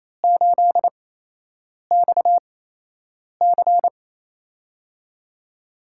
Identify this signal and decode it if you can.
Morse code, heard as a keyed tone, about 28 wpm, spaced as for 8 wpm, 700 Hz. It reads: 8XC
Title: USA Call Signs - 4 Characters - 8x Character Spacing - 28wpm